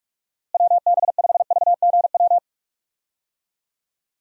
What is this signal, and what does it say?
Morse code, heard as a keyed tone, about 41 wpm, 700 Hz. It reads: WB5VGW
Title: USA Call Signs - 6 Characters - Repeated 3x - 45wpm